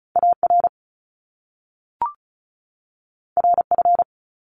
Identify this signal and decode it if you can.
Morse code, heard as a keyed tone, about 36 wpm, 700 Hz. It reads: AR RF
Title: Top Two Letter Combinations in Words 35wpm